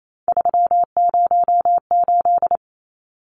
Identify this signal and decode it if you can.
Morse code, heard as a keyed tone, about 28 wpm, 700 Hz. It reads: 308
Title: Numbers - 3-Digit 28wpm